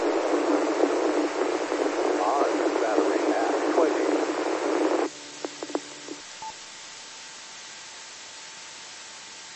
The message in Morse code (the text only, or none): none